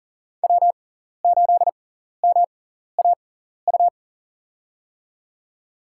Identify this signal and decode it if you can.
Morse code, heard as a keyed tone, about 40 wpm, spaced as for 14 wpm, 700 Hz. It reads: W8MAU
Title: USA Call Signs - 5 Characters - 6x Character Spacing - 40wpm